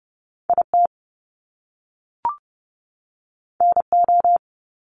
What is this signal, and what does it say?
Morse code, heard as a keyed tone, about 30 wpm, 700 Hz. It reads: IT NO
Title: Top Two Letter Combinations in Words 30wpm